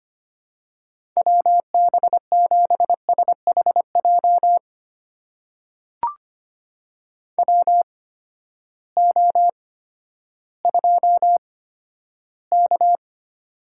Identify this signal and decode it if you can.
Morse code, heard as a keyed tone, about 25 wpm, spaced as for 13 wpm, 700 Hz. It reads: WB7SHJ WO2K
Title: USA Call Signs - 8x Character Spacing - 25wpm